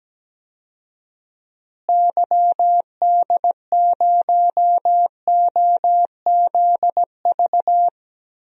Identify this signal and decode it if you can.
Morse code, heard as a keyed tone, about 17 wpm, 700 Hz. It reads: YD0OZV
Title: DXCC-100 Call Signs - 6 Characters - Repeated 2x - 17wpm